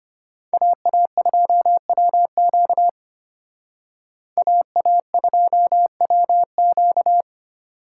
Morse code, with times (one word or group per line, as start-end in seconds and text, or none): 0.5-2.9 AA2WQ
4.4-7.2 AA2WQ